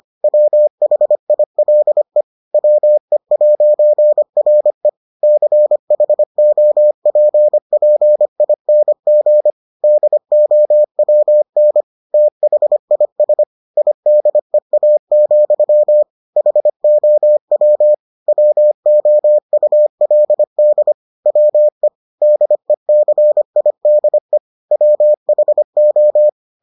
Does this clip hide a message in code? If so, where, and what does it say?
Yes: Morse code from 0.2 to 26.3 s, heard as a keyed tone, about 25 wpm, 600 Hz. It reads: WHILE WE'RE CHOPPING DOWN THIS IDEA, HOW WOULD WE DECIDE WHO